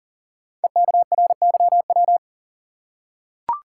Morse code, heard as a keyed tone, about 39 wpm, 690 Hz. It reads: EKRYW
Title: Code Groups - 5 Letters - 40wpm